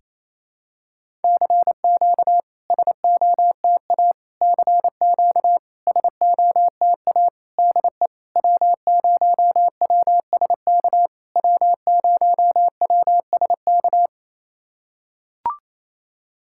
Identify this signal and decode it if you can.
Morse code, heard as a keyed tone, about 28 wpm, 700 Hz. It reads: CQ SOTA CQ SOTA DE W0WSK W0WSK
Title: SOTA QSO Practice 28wpm